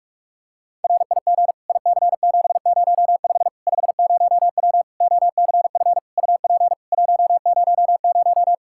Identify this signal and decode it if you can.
Morse code, heard as a keyed tone, about 44 wpm, 700 Hz. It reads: RIG IC705 50W OCF UP 100
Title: Practice QSO - Exchange 2 - Info - 45wpm